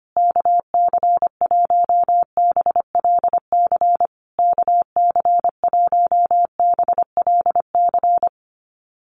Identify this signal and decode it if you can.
Morse code, heard as a keyed tone, about 25 wpm, 700 Hz. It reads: KC1BLC KC1BLC
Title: USA Call Signs Repeated 3x 25wpm